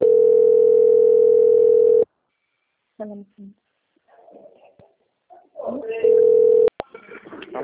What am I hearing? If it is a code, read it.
Ringback tone (North America), 440 + 480 Hz, with speech.
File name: out- - - - - .25746